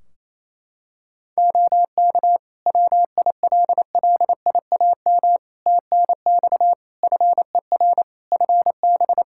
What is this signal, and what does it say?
Morse code, heard as a keyed tone, about 28 wpm, 700 Hz. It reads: OK WILLIAM TNX FER FB
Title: Practice QSO - Exchange 4 - Ending - 28wpm